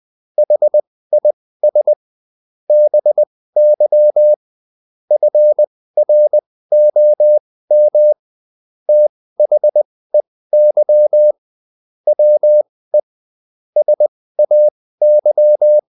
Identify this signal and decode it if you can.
Morse code, heard as a keyed tone, about 20 wpm, spaced as for 15 wpm, 600 Hz. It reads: HIS BY FROM THEY WE SAY
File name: Top-100-Words-15